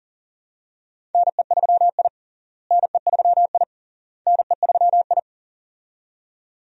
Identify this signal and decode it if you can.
Morse code, heard as a keyed tone, about 40 wpm, 700 Hz. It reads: NE3I NE3I NE3I
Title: USA Call Signs - 4 Characters - Repeated 3x - 40wpm